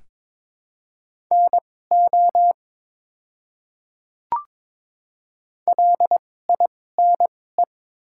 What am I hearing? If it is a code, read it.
Morse code, heard as a keyed tone, about 22 wpm, spaced as for 16 wpm, 700 Hz. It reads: NO LINE
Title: 2x Character Spacing - Top 500 Words - 22wpm